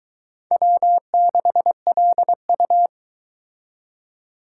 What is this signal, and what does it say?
Morse code, heard as a keyed tone, about 23 wpm, 700 Hz. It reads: W6LU